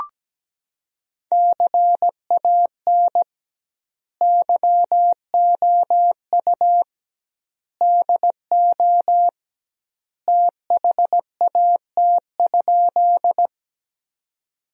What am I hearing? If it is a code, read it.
Morse code, heard as a keyed tone, about 17 wpm, 700 Hz. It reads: CAN YOU DO THAT?
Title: 2x Word-Spacing - Sentences from Top 100 Words - 17wpm